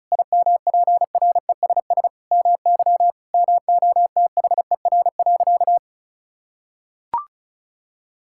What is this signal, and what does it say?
Morse code, heard as a keyed tone, about 36 wpm, 700 Hz. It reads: IMPRESS MY MOTHER.
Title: Sentences from Top 1600 Words 35wpm